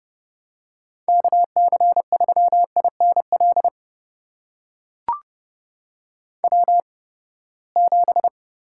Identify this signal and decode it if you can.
Morse code, heard as a keyed tone, about 31 wpm, spaced as for 21 wpm, 700 Hz. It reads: KC3INL W7
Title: USA Call Signs - 8x Character Spacing - 30wpm